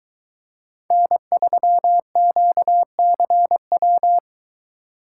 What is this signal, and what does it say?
Morse code, heard as a keyed tone, about 22 wpm, 700 Hz. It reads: N3QCW